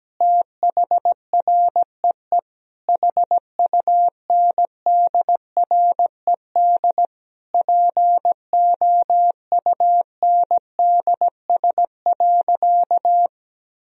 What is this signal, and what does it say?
Morse code, heard as a keyed tone, about 17 wpm, 700 Hz. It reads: THREE HUNDRED POUNDS.
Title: Sentences from Top 1000 Words - Review Part 4 - 17wpm